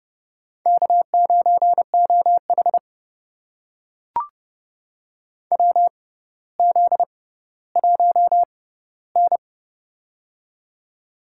Morse code, heard as a keyed tone, about 30 wpm, spaced as for 15 wpm, 700 Hz. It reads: K9OH WZ1N
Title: USA Call Signs - 4 Characters - 6x Character Spacing - 30wpm